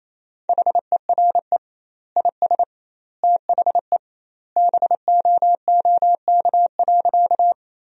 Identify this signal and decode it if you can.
Morse code, heard as a keyed tone, about 28 wpm, 700 Hz. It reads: HERE IS THE BOOK.
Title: 2x Word-Spacing - Sentences from Top 300 Words - 28wpm